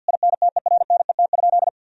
Morse code, heard as a keyed tone, about 50 wpm, 700 Hz. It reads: INTERNET?
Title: Sentences from Top 1000 Words - Review Part 4 - 50wpm